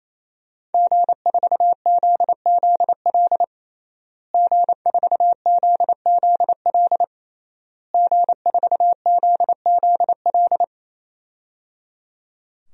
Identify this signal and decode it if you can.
Morse code, heard as a keyed tone, about 28 wpm, 700 Hz. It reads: G4ZZL G4ZZL G4ZZL
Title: DXCC-100 Call Signs - 5 Characters - Repeated 3x - 28wpm